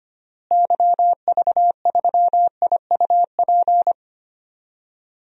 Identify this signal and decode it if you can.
Morse code, heard as a keyed tone, about 25 wpm, 700 Hz. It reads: YV3IUP
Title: DXCC-100 Call Signs - 6 Characters - Repeated 3x - 25wpm